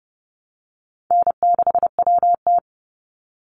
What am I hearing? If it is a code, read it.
Morse code, heard as a keyed tone, about 30 wpm, 700 Hz. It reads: N6WT
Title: NA SOTA Chaser Callsigns 30wpm